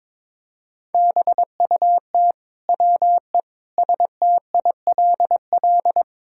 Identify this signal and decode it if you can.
Morse code, heard as a keyed tone, about 22 wpm, 700 Hz. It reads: BUT WE STILL
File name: Sentences with 9-Word Phrases - 22wpm